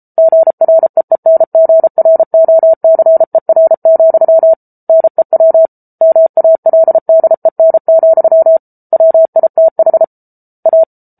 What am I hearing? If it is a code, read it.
Morse code, heard as a keyed tone, about 34 wpm, 650 Hz. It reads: GREENGROCER, NEW MALDEN, WITH A